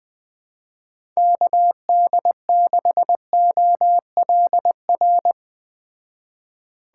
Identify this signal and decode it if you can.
Morse code, heard as a keyed tone, about 20 wpm, 700 Hz. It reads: KD6OLR
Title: Speed Racing - USA 2x3 Call Signs 17wpm